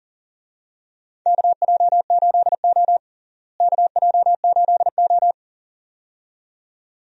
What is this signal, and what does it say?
Morse code, heard as a keyed tone, about 39 wpm, 700 Hz. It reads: KJ8O KJ8O
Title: USA Call Signs - 4 Characters - Repeated 2x - 40wpm